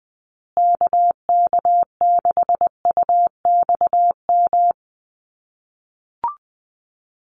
Morse code, heard as a keyed tone, about 20 wpm, 700 Hz. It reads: KK6UXM
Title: USA Call Signs Repeated 2x 20wpm